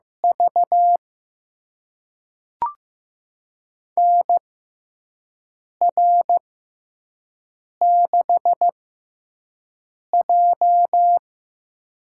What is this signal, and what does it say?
Morse code, heard as a keyed tone, about 15 wpm, spaced as for 5 wpm, 700 Hz. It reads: V NR6J
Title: USA Call Signs - 4 Characters - 6x Character Spacing - 15wpm